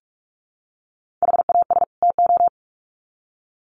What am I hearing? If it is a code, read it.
Morse code, heard as a keyed tone, about 44 wpm, 700 Hz. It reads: HAS TO
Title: Sets of 2 Words 45wpm